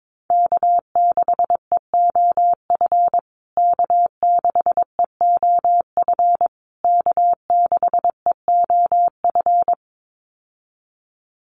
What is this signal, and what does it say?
Morse code, heard as a keyed tone, about 22 wpm, 700 Hz. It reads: K6EOF K6EOF K6EOF
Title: USA Call Signs Repeated 3x 22wpm